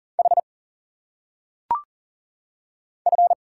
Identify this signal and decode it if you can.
Morse code, heard as a keyed tone, about 40 wpm, 700 Hz. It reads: H F
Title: Single Letters - Phonetic Pronunciation 40wpm